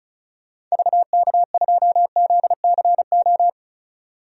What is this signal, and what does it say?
Morse code, heard as a keyed tone, about 36 wpm, 700 Hz. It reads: VK2ZCO
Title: DXCC-100 Call Signs - 6 Characters - 35wpm